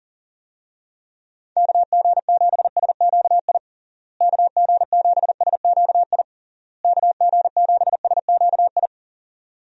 Morse code, heard as a keyed tone, about 40 wpm, 700 Hz. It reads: KG7SQI KG7SQI KG7SQI
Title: USA Call Signs - 6 Characters - Repeated 3x - 40wpm